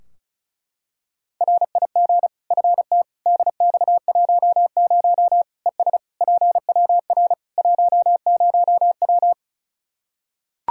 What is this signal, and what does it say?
Morse code, heard as a keyed tone, about 34 wpm, 700 Hz. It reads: RIG FT DX10 ES PWR 10W
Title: Practice QSO - Exchange 2 - Info - 35wpm